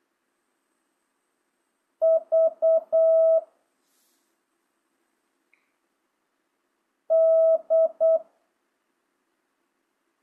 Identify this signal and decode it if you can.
Morse code, heard as a keyed tone, about 8 wpm, 650 Hz. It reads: V D